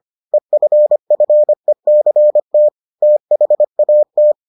Morse code, heard as a keyed tone, about 25 wpm, 600 Hz. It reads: EFFECT THAT